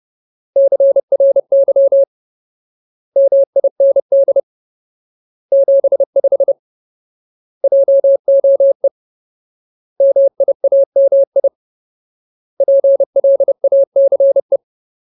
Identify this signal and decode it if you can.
Morse code, heard as a keyed tone, about 30 wpm, 550 Hz. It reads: CRY MIND 75 JOE MIAMI PLACE
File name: Ultimate Word List / 30WPM Ultimate